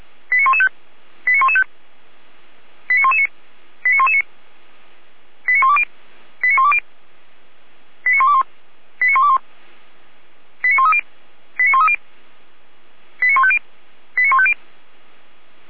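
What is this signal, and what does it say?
Five-tone selective call (ZVEI), heard as quick runs of five tones, tones of about 70 ms. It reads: 89115 89115 89119 89119 89120 89120 89121 89121 89130 89130 89150 89150
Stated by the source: ZVEI five-tone selective call received by SDR.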